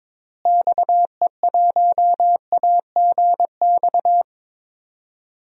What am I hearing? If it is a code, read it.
Morse code, heard as a keyed tone, about 21 wpm, 700 Hz. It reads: XE1AGX
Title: DXCC-100 Call Signs - 6 Characters - Repeated 2x - 22wpm